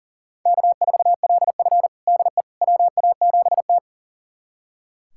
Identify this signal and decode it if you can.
Morse code, heard as a keyed tone, about 37 wpm, 700 Hz. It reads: K4LF DE WA7T
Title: USA Callsign DE Callsign - 4 Characters - 40wpm